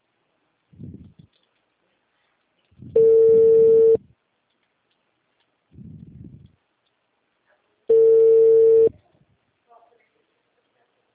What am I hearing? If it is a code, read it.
Ringback tone (China, 450 Hz).
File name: out- -2008- - - .4253